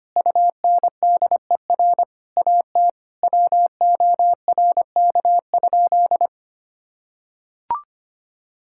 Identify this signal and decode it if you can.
Morse code, heard as a keyed tone, about 25 wpm, 700 Hz. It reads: UNDER AT WORK?
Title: Sentences from Top 1000 Words - Review Part 6 - 25wpm